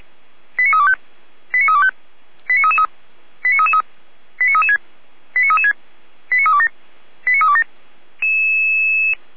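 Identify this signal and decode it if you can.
Five-tone selective call (ZVEI), heard as quick runs of five tones, tones of about 70 ms. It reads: 89325 89325 89333 89333 89336 89336 89327 89327
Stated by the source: ZVEI five-tone selective call 89320 received on SDR.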